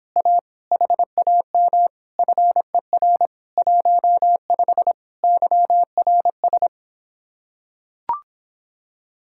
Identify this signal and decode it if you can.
Morse code, heard as a keyed tone, about 26 wpm, 700 Hz. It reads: A HAM FER 15 YRS